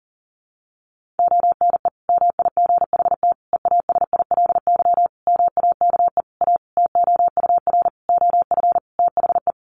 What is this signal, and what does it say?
Morse code, heard as a keyed tone, about 40 wpm, 700 Hz. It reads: ONE MIGHT EASILY MAKE A TOUR OF THE
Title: Sentences from Top 1400 Words 40wpm